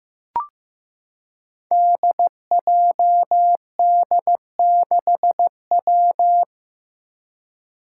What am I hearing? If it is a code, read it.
Morse code, heard as a keyed tone, about 15 wpm, 700 Hz. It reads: DJD6W
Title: Five Challenging Letter-Number Combinations 15wpm